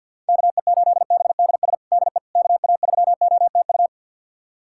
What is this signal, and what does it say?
Morse code, heard as a keyed tone, about 50 wpm, 700 Hz. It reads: KE8BDS DE KA3OTU
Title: USA Callsign DE Callsign 50wpm